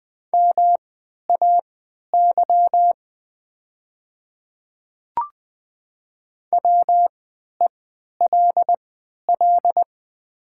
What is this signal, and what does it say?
Morse code, heard as a keyed tone, about 20 wpm, spaced as for 11 wpm, 700 Hz. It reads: MAY WELL
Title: 3x Character Spacing - Top 200 Words - 20wpm